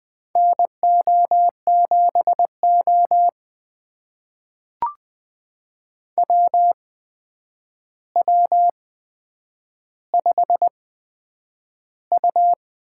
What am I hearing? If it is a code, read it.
Morse code, heard as a keyed tone, about 20 wpm, spaced as for 9 wpm, 700 Hz. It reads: NO7O WW5U
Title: USA Call Signs - 4 Characters - 8x Character Spacing - 20wpm